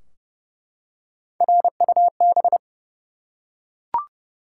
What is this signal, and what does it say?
Morse code, heard as a keyed tone, about 30 wpm, 700 Hz. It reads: RUB
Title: Three Letter Words - 30wpm